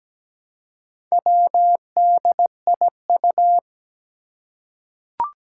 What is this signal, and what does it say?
Morse code, heard as a keyed tone, about 17 wpm, 700 Hz. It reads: WDIU
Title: Four Challenging Letter-Number Combinations 17wpm